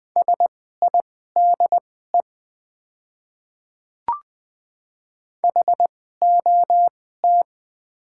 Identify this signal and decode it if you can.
Morse code, heard as a keyed tone, about 20 wpm, spaced as for 14 wpm, 700 Hz. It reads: SIDE HOT
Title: 2x Character Spacing - Top 100 Words - 20wpm